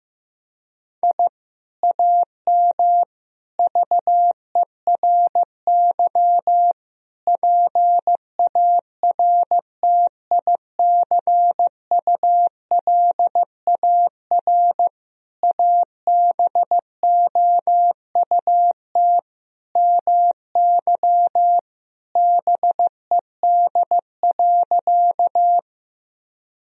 Morse code, heard as a keyed tone, about 15 wpm, 700 Hz. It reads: I AM VERY PARTICULAR ABOUT MY BED.
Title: Sentences from Top 1000 Words - Review Part 4 - 15wpm